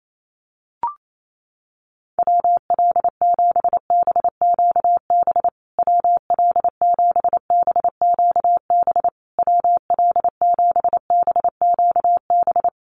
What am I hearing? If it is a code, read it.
Morse code, heard as a keyed tone, about 28 wpm, 690 Hz. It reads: WL7BQB WL7BQB WL7BQB
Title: USA Call Signs Repeated 3x 28wpm